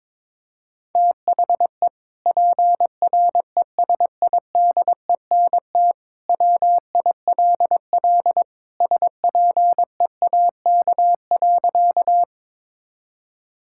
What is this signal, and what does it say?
Morse code, heard as a keyed tone, about 23 wpm, 700 Hz. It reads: THE PRESIDENT WILL SPEAK.
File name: practice / 22wpm / Sentences with 2-Word Phrases - 22wpm